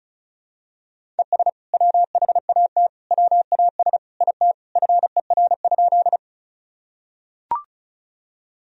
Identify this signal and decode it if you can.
Morse code, heard as a keyed tone, about 36 wpm, 700 Hz. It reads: ES WHAT WAS IT FER?
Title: Sentences from Top 100 Words and Abbreviations 35wpm